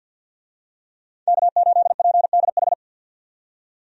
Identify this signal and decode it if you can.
Morse code, heard as a keyed tone, about 52 wpm, 700 Hz. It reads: K8PDH
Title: USA Call Signs - 5 Characters - 50wpm